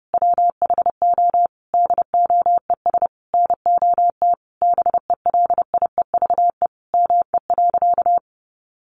Morse code, heard as a keyed tone, about 30 wpm, 700 Hz. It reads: WHO DOES NOT BELIEVE ME.
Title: Sentences from Top 1000 Words 30wpm